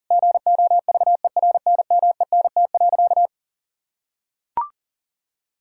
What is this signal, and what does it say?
Morse code, heard as a keyed tone, about 40 wpm, 700 Hz. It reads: GOVERNMENT.
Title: Sentences from Top 1000 Words - Review Part 5 - 40wpm